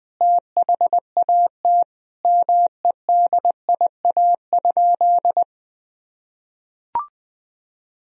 Morse code, heard as a keyed tone, about 20 wpm, 700 Hz. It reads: THAT MEDIA?
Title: Sentences from Top 1000 Words - Review Part 4 - 20wpm